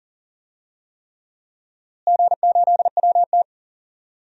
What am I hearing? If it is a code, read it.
Morse code, heard as a keyed tone, about 40 wpm, 700 Hz. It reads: G8WT
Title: DXCC-100 Call Signs - 4 Characters - Repeated 3x - 40wpm